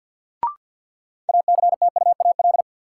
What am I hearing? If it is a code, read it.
Morse code, heard as a keyed tone, about 52 wpm, 700 Hz. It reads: ACTUAL
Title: Sentences from Top 1000 Words - Review Part 5 - New Word Drill - 50wpm